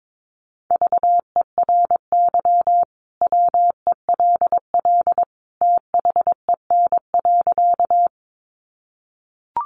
Morse code, heard as a keyed tone, about 22 wpm, 700 Hz. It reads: VERY WELL THEN.
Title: Sentences from Top 200 Words 22wpm